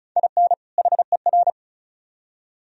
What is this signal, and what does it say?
Morse code, heard as a keyed tone, about 46 wpm, 700 Hz. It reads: IN HER